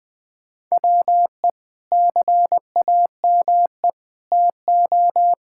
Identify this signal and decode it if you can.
Morse code, heard as a keyed tone, about 21 wpm, 700 Hz. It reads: WE CAME TO
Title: Sentences from Top 1000 Words - Review Part 6 - 20wpm